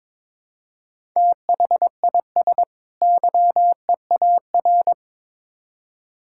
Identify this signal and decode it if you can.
Morse code, heard as a keyed tone, about 22 wpm, 700 Hz. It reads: THIS YEAR
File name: Build A Phrase - 2-Word Phrases - 22wpm